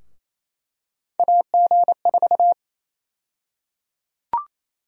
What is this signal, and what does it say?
Morse code, heard as a keyed tone, about 28 wpm, 700 Hz. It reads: AG4
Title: USA Call Sign Prefixes - 28wpm